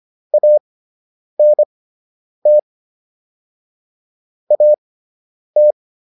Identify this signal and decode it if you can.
Morse code, heard as a keyed tone, about 25 wpm, spaced as for 9 wpm, 600 Hz. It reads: ANT AT